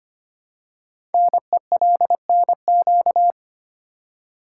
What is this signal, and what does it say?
Morse code, heard as a keyed tone, about 25 wpm, 700 Hz. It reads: NELNQ
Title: Code Groups - 5 Letters - 25wpm